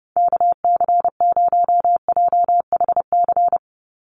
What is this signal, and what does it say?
Morse code, heard as a keyed tone, about 30 wpm, 700 Hz. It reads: KC0JHC
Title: USA Call Signs 30wpm